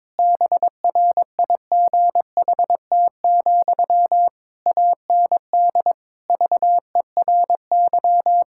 Morse code, heard as a keyed tone, about 22 wpm, 700 Hz. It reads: BRIGHT, AND VERY
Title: Sentences from Top 1600 Words 22wpm